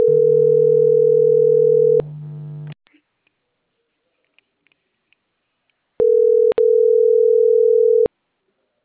Ringback tone (North America), 440 + 480 Hz.